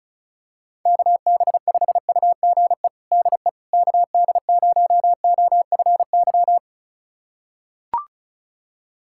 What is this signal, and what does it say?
Morse code, heard as a keyed tone, about 36 wpm, 700 Hz. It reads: KB5UGE DE KD0OFY
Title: USA Callsign DE Callsign 35wpm